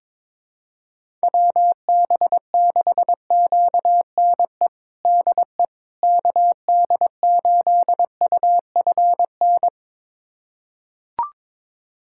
Morse code, heard as a keyed tone, about 22 wpm, 700 Hz. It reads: WB6QNE DE KD8UFN
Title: USA Callsign DE Callsign - 6 Characters - 22wpm